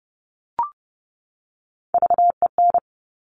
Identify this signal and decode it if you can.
Morse code, heard as a keyed tone, about 30 wpm, 700 Hz. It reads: VEN
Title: Top Three Letter Combinations in Words 30wpm